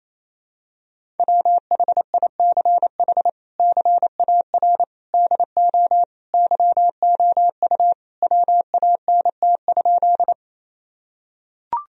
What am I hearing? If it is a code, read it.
Morse code, heard as a keyed tone, about 28 wpm, 700 Hz. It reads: WHICH CAR DO YOU WANT?
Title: Sentences from Top 1000 Words - Review Part 1 - 28wpm